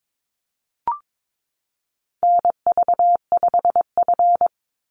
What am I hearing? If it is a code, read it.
Morse code, heard as a keyed tone, about 22 wpm, 700 Hz. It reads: NV5F
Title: Call Signs - 22wpm